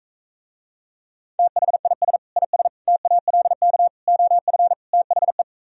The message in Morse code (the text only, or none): THIS IS TALK OF THE